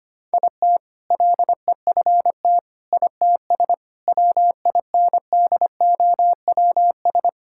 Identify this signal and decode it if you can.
Morse code, heard as a keyed tone, about 25 wpm, 700 Hz. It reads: IT LEFT ITS WINDOWS
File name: practice / 25wpm / All-Age Appropriate Jokes - 25wpm